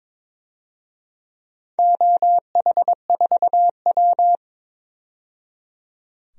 Morse code, heard as a keyed tone, about 22 wpm, 700 Hz. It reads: OH4W
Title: DXCC-100 Call Signs - 4 Characters - 22wpm